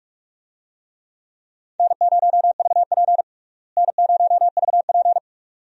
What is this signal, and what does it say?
Morse code, heard as a keyed tone, about 46 wpm, 700 Hz. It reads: N0VP N0VP
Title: USA Call Signs - 4 Characters - Repeated 2x - 45wpm